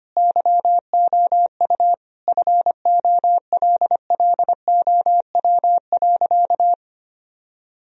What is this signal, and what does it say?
Morse code, heard as a keyed tone, about 25 wpm, 700 Hz. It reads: YOU FOLLOW.
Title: Sentences from Top 1000 Words - Review Part 3 - 25wpm